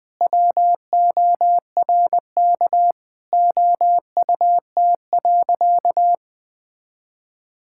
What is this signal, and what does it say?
Morse code, heard as a keyed tone, about 21 wpm, 700 Hz. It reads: WORK OUT.
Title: Sentences with 3-Word Phrases - 20wpm